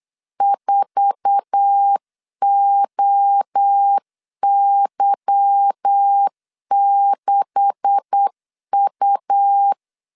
Morse code, heard as a keyed tone, about 8 wpm, 800 Hz. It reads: 4OY6U
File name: CW / DET / 1 9 1500C S A 8wpm 5637